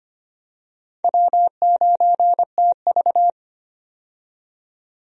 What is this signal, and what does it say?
Morse code, heard as a keyed tone, about 25 wpm, 700 Hz. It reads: W9TV
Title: USA Call Signs - 4 Characters - 25wpm